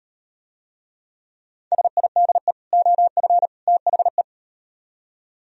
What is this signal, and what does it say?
Morse code, heard as a keyed tone, about 38 wpm, 700 Hz. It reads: SIDE OF THE